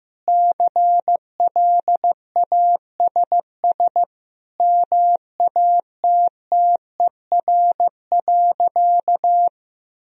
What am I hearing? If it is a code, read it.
Morse code, heard as a keyed tone, about 15 wpm, 700 Hz. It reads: CLASS MATTER.
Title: Sentences from Top 1000 Words - Review Part 2 - 15wpm